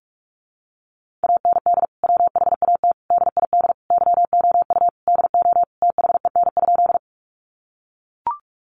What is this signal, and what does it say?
Morse code, heard as a keyed tone, about 44 wpm, 700 Hz. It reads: AND WHAT DID YOU DO THEN?
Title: Sentences from Top 100 Words 45wpm